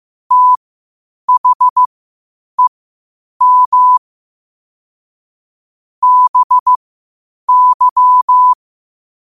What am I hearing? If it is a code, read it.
Morse code, heard as a keyed tone, about 15 wpm, spaced as for 8 wpm, 1000 Hz. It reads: THEM BY
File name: qotdfiles / 8wpm